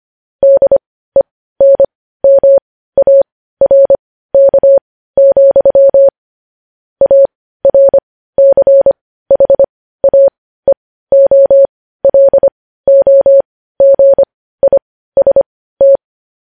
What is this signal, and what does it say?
Morse code, heard as a keyed tone, about 25 wpm, spaced as for 15 wpm, 550 Hz. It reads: DENMARK, ARCHAEOLOGIST
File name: podcast / News headlines-1x-25x15